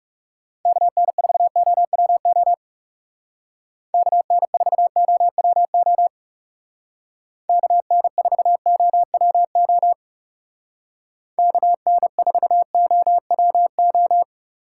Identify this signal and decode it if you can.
Morse code, heard as a keyed tone, about 36 wpm, 700 Hz. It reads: KN4OWO KN4OWO KN4OWO KN4OWO
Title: Speed Racing - USA 2x3 Call Signs 30wpm